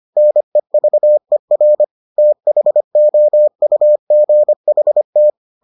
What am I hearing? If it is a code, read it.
Morse code, heard as a keyed tone, about 25 wpm, 600 Hz. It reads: NEVER THOUGHT